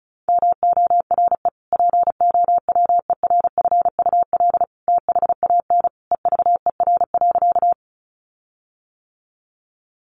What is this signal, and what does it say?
Morse code, heard as a keyed tone, about 34 wpm, 700 Hz. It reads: MORE POWERFUL THAN EVER.
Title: Sentences from Top 1300 Words 35wpm